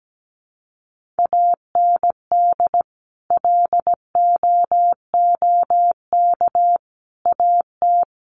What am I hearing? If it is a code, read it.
Morse code, heard as a keyed tone, about 17 wpm, 700 Hz. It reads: AND LOOK AT
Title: Sentences from Top 1500 Words 17wpm